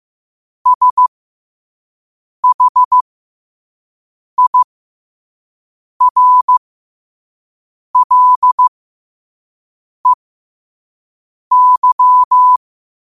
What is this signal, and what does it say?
Morse code, heard as a keyed tone, about 15 wpm, spaced as for 5 wpm, 1000 Hz. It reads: SHIRLEY
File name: 5wpm